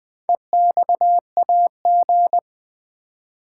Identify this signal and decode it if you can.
Morse code, heard as a keyed tone, about 20 wpm, 700 Hz. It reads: EXAG
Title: Build A Word - 4 Letters At A Time - 10-Letter Words - 20wpm